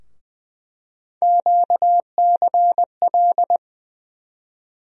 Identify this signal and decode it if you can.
Morse code, heard as a keyed tone, about 20 wpm, 700 Hz. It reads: QCL
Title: USA Call Sign Postfixes - 3 Letters 20wpm